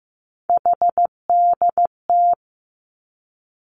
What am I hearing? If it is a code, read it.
Morse code, heard as a keyed tone, about 15 wpm, 700 Hz. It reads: HDT
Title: Three Challenging Letter-Number Combinations 15wpm